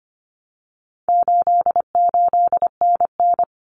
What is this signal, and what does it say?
Morse code, heard as a keyed tone, about 25 wpm, 700 Hz. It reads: 88NN
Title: Common Cut Numbers - 4-Digit 25wpm